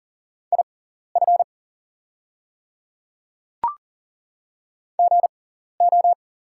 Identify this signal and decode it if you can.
Morse code, heard as a keyed tone, about 38 wpm, spaced as for 14 wpm, 690 Hz. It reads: IF GO
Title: 6x Character Spacing - Top 100 Words - 40wpm